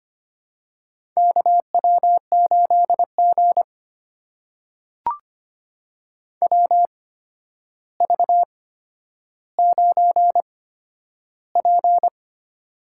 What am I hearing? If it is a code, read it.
Morse code, heard as a keyed tone, about 25 wpm, spaced as for 11 wpm, 700 Hz. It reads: KW8G WV9P
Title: USA Call Signs - 4 Characters - 8x Character Spacing - 25wpm